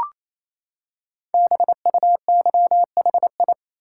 Morse code, heard as a keyed tone, about 27 wpm, 700 Hz. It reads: BUYHI
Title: Code Groups - 5 Letters - 28wpm